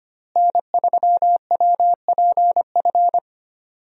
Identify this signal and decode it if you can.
Morse code, heard as a keyed tone, about 25 wpm, 700 Hz. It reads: N3WPF
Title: USA Call Signs - 5 Characters - Repeated 3x - 25wpm